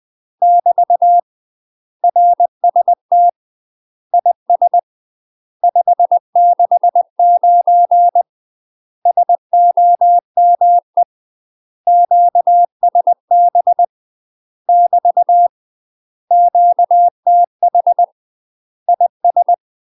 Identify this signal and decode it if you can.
Morse code, heard as a keyed tone, about 20 wpm, 700 Hz. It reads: <BT> RST IS 569 SOME QSB <BT> QTH IS
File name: UK MASTER QSOs TRAINING / 20WPM / UK MASTER QSOs 0005